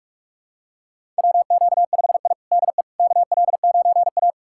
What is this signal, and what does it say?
Morse code, heard as a keyed tone, about 46 wpm, 700 Hz. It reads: WQ5I DE KL9A
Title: USA Callsign DE Callsign - 4 Characters - 45wpm